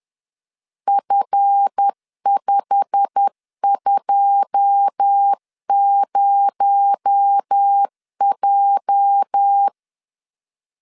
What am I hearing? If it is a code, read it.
Morse code, heard as a keyed tone, about 10 wpm, 800 Hz. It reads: F520J
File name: LEZIONI / 9 2 10wpm 2736